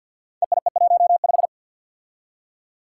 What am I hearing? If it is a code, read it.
Morse code, heard as a keyed tone, about 47 wpm, 700 Hz. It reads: EIE15
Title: Five Challenging Letter-Number Combinations 50wpm